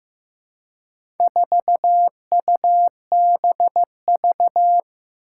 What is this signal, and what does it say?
Morse code, heard as a keyed tone, about 15 wpm, 700 Hz. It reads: 4UBV
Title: Four Challenging Letter-Number Combinations 15wpm